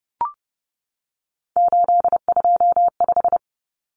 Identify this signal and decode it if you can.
Morse code, heard as a keyed tone, about 28 wpm, 700 Hz. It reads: 825
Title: Common Cut Numbers - 3-Digit 30wpm